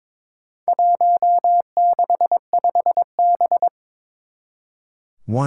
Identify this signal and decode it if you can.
Morse code, heard as a keyed tone, about 22 wpm, 700 Hz. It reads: 165B
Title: Four Challenging Letter-Number Combinations 22wpm